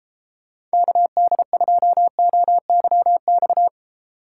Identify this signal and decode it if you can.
Morse code, heard as a keyed tone, about 33 wpm, 700 Hz. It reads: KD2OYX